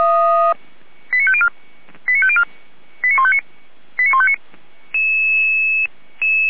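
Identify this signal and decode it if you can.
Five-tone selective call (ZVEI), heard as quick runs of five tones, tones of about 70 ms. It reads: 89504 89504 89159 89159
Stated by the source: ZVEI five-tone call received on SDR, from the audio server of Feuerwehr Orlamünde.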